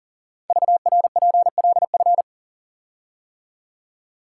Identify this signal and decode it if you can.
Morse code, heard as a keyed tone, about 39 wpm, 690 Hz. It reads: VRPLF
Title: Code Groups - 5 Letters - 40wpm